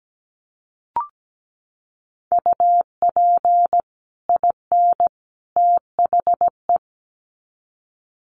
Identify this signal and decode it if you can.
Morse code, heard as a keyed tone, about 17 wpm, 700 Hz. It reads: UP IN THE
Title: Sets of 3 Words 17wpm